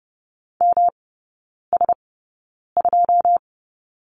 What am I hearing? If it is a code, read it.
Morse code, heard as a keyed tone, about 30 wpm, spaced as for 9 wpm, 700 Hz. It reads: MS2
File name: Lesson 18e - Letter m - Three Characters 30wpm